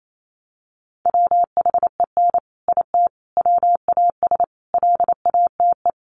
Morse code, heard as a keyed tone, about 28 wpm, 700 Hz. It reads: WHEN IT WAS LATE
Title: Sentences from Top 600 Words 28wpm